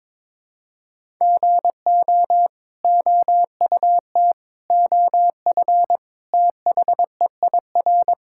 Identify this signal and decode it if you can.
Morse code, heard as a keyed tone, about 22 wpm, 700 Hz. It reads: GO OUT OF THEIR
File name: practice / 22wpm / Sets of 6 Words - 22wpm